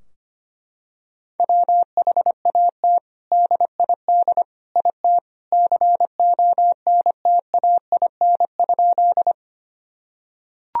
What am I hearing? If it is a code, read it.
Morse code, heard as a keyed tone, about 25 wpm, 700 Hz. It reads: WHAT DID IT CONTAIN?
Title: Sentences from Top 1000 Words - Review Part 2 - 25wpm